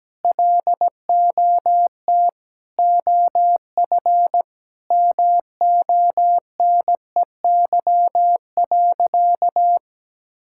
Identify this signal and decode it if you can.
Morse code, heard as a keyed tone, about 17 wpm, 700 Hz. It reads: LOT OF MONEY.
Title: Sentences from Top 1000 Words - Review Part 5 - 17wpm